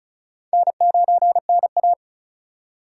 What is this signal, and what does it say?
Morse code, heard as a keyed tone, about 36 wpm, 700 Hz. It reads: N9NA
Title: USA Call Signs - 4 Characters - 3x Character Spacing - 35wpm